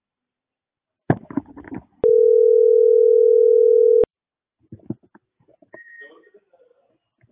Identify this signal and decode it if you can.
Ringback tone (North America), 440 + 480 Hz.